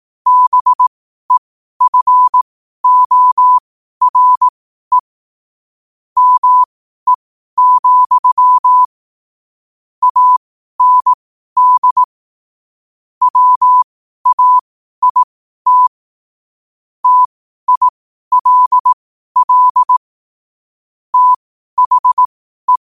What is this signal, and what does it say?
Morse code, heard as a keyed tone, about 18 wpm, spaced as for 12 wpm, 1000 Hz. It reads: BEFORE ME, AND WAIT TILL THE
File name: qotdfiles / 12wpm